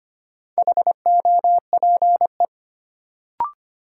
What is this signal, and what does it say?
Morse code, heard as a keyed tone, about 25 wpm, 700 Hz. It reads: HOPE
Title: Sentences from Top 1000 Words - New Word Drill - 25wpm